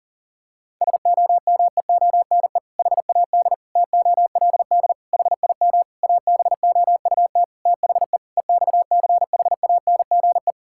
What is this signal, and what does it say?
Morse code, heard as a keyed tone, about 40 wpm, 690 Hz. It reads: SOMEONE HAD TOLD HIM ABOUT THE EXCHANGE
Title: Sentences from Top 1000 Words - Review Part 6 - 40wpm